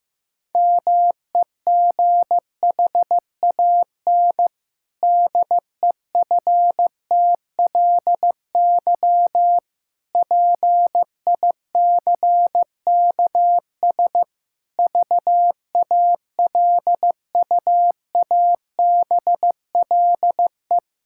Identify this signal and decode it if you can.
Morse code, heard as a keyed tone, about 15 wpm, 700 Hz. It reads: MEGHAN DEFTLY PICKS VALUABLE
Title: Pangram Challenge v2 - 15wpm